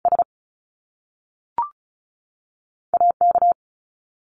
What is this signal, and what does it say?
Morse code, heard as a keyed tone, about 36 wpm, 700 Hz. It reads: S A K
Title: Top Two Letter Combinations in Words 35wpm